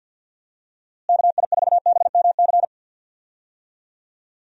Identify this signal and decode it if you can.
Morse code, heard as a keyed tone, about 50 wpm, 700 Hz. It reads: KI4BMC
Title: USA Call Signs - 6 Characters - Repeated 3x - 50wpm